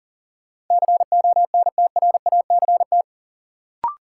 Morse code, heard as a keyed tone, about 40 wpm, 700 Hz. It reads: CONTRACT
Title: Sentences from Top 1400 Words - New Word Drill - 40wpm